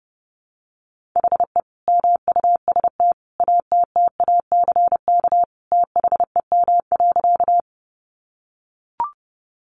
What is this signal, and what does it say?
Morse code, heard as a keyed tone, about 30 wpm, 700 Hz. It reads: HE MUST ATTACK THEM.
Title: Sentences from Top 1500 Words 30wpm